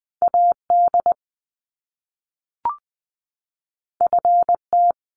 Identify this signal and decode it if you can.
Morse code, heard as a keyed tone, about 19 wpm, 700 Hz. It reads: AD FT
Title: Top Two Letter Combinations in Words 20wpm